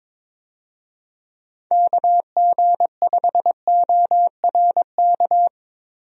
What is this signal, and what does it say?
Morse code, heard as a keyed tone, about 22 wpm, 700 Hz. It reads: KG5ORK
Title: USA Call Signs - 6 Characters - Repeated 2x - 22wpm